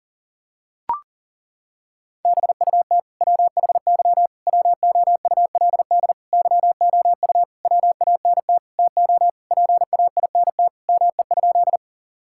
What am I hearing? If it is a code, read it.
Morse code, heard as a keyed tone, about 40 wpm, 700 Hz. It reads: BUT WHY WOULD YOU WANT TO PAINT ME?
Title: Sentences from Top 1000 Words - Review Part 6 - 40wpm